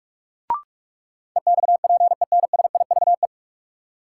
Morse code, heard as a keyed tone, about 42 wpm, 700 Hz. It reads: EXPENSIVE
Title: Sentences from Top 1500 Words - New Word Drill Alt - 45wpm